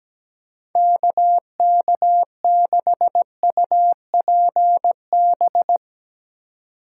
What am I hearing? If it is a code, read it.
Morse code, heard as a keyed tone, about 17 wpm, 700 Hz. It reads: KK6UPB
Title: Speed Racing - USA 2x3 Call Signs 17wpm